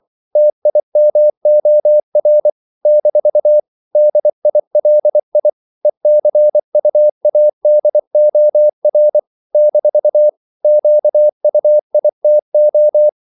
Morse code, heard as a keyed tone, about 24 wpm, 600 Hz. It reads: TIMOR - DILI ECUADOR - QUITO
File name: CW / Practice / NationalCapitals24